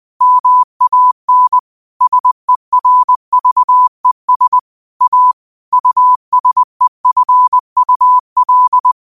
Morse code, heard as a keyed tone, about 20 wpm, 1000 Hz. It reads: MAN SERVES A USEFUL
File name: qotdfiles / 20wpm